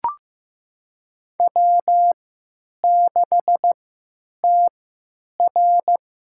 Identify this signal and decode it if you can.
Morse code, heard as a keyed tone, about 15 wpm, spaced as for 9 wpm, 700 Hz. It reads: W6TR
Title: USA Call Signs - 3x Character Spacing - 15wpm